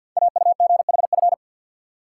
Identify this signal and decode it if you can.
Morse code, heard as a keyed tone, about 50 wpm, 690 Hz. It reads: AUGHF
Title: Code Groups - 5 Letters - 50wpm